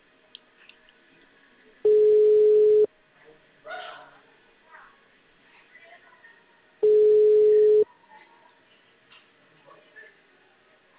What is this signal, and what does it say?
Ringback tone (Europe, 425 Hz), with speech.